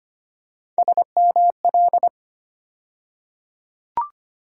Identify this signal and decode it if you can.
Morse code, heard as a keyed tone, about 25 wpm, 700 Hz. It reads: SML
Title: USA Call Sign Postfixes - 3 Letters 25wpm